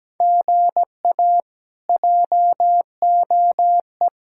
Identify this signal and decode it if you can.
Morse code, heard as a keyed tone, about 17 wpm, 700 Hz. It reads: GA JOE
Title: K1USN SST QSO Practice - 17wpm